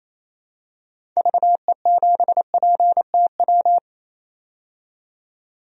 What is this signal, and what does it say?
Morse code, heard as a keyed tone, about 28 wpm, 700 Hz. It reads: VE7PTW
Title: DXCC-100 Call Signs - 6 Characters - 28wpm